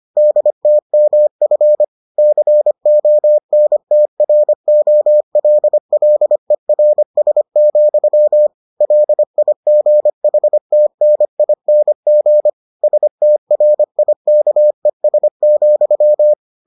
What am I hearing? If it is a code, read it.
Morse code, heard as a keyed tone, about 25 wpm, 600 Hz. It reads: DTMF CONTROLLERS, LIGHTNING STRIKES,